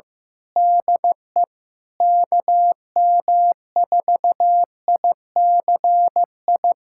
Morse code, heard as a keyed tone, about 16 wpm, 700 Hz. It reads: DE KM4ICI
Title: USA Callsign DE Callsign 15wpm